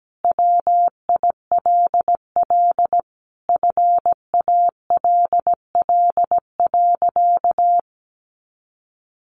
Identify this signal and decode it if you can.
Morse code, heard as a keyed tone, about 17 wpm, 700 Hz. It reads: WILL FALL.
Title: Sentences from Top 500 Words 17wpm